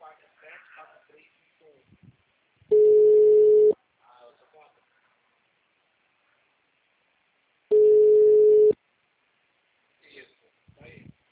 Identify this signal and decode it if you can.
Ringback tone (Europe, 425 Hz), with speech.